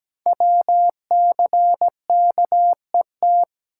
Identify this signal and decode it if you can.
Morse code, heard as a keyed tone, about 17 wpm, 700 Hz. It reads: WCKET
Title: Code Groups - 5 Letters - 17wpm